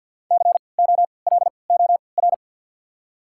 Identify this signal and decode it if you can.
Morse code, heard as a keyed tone, about 53 wpm, spaced as for 29 wpm, 690 Hz. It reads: COLOR
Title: 3x Character Spacing - Top 500 Words - 50wpm